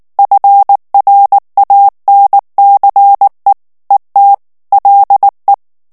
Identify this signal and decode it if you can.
Morse code, heard as a keyed tone, about 19 wpm, 800 Hz. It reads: FRANCE ET LE
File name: FRANCAIS TEXTE CLAIR 1 VITESSE 18 MOTS MINUTES